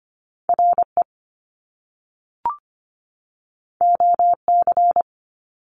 Morse code, heard as a keyed tone, about 25 wpm, 700 Hz. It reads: RE OC
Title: Top Two Letter Combinations in Words 25wpm